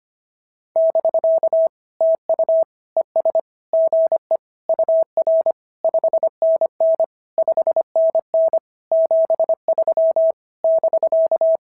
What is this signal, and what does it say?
Morse code, heard as a keyed tone, about 24 wpm, 650 Hz. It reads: <BK> TU ES GE UR 5NN 5NN 73 <BK>
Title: SOTA QSO Practice 25wpm